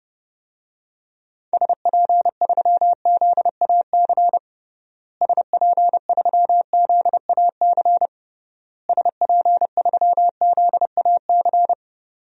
Morse code, heard as a keyed tone, about 30 wpm, 700 Hz. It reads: SP3ZAC SP3ZAC SP3ZAC
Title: DXCC-100 Call Signs - 6 Characters - Repeated 3x - 30wpm